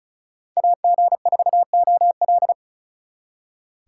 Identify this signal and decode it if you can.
Morse code, heard as a keyed tone, about 44 wpm, 700 Hz. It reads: AG4OL